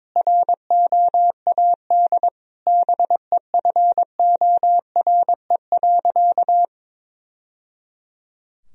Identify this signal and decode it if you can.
Morse code, heard as a keyed tone, about 22 wpm, 700 Hz. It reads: ROAD BEFORE.
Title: Sentences from Top 1000 Words - Review Part 2 - 22wpm